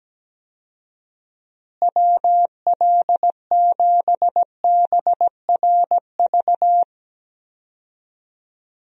Morse code, heard as a keyed tone, about 17 wpm, 700 Hz. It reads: WL7BRV
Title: USA Call Signs - 6 Characters - Repeated 2x - 17wpm